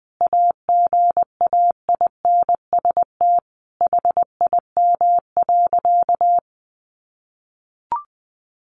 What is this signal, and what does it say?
Morse code, heard as a keyed tone, about 20 wpm, 700 Hz. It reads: AGAINST HIM.
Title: Sentences from Top 1300 Words 20wpm